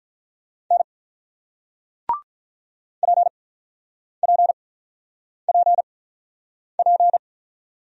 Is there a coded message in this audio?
Yes: Morse code from 0.7 to 7.2 s, heard as a keyed tone, about 48 wpm, spaced as for 9 wpm, 700 Hz. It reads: N PPPP